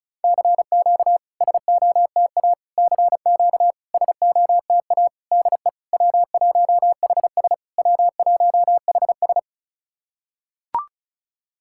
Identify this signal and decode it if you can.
Morse code, heard as a keyed tone, about 36 wpm, 700 Hz. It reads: CQ SOTA CQ SOTA DE W1HS W1HS
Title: SOTA QSO Practice 35wpm